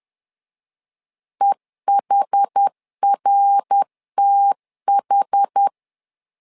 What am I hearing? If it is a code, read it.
Morse code, heard as a keyed tone, about 10 wpm, 800 Hz. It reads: EHRTH
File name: LEZIONI / 4 Q 10wpm 2210